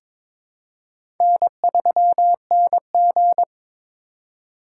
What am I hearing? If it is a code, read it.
Morse code, heard as a keyed tone, about 22 wpm, 700 Hz. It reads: N3NG
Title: USA Call Signs - 2x Character Spacing - 22wpm